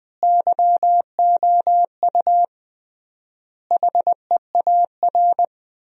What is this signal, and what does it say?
Morse code, heard as a keyed tone, about 20 wpm, 700 Hz. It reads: YOU HEAR
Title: 3x Word-Spacing - Sentences from Top 300 Words - 20wpm